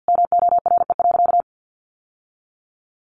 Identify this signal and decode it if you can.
Morse code, heard as a keyed tone, about 50 wpm, 700 Hz. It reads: MORE.
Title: Sentences from Top 700 Words 50wpm